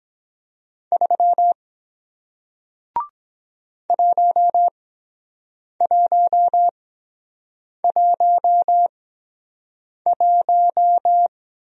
Morse code, heard as a keyed tone, about 23 wpm, spaced as for 7 wpm, 700 Hz. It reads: 3 1111